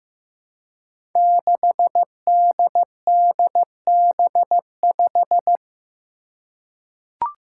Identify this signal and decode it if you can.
Morse code, heard as a keyed tone, about 15 wpm, 700 Hz. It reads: 6DDB5
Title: Five Challenging Letter-Number Combinations 15wpm